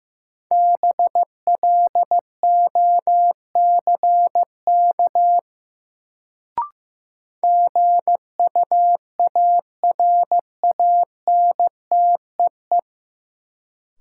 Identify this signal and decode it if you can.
Morse code, heard as a keyed tone, about 15 wpm, 700 Hz. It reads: BLOCK GUARANTEE
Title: Sentences from Top 1600 Words - New Word Drill Alt - 15wpm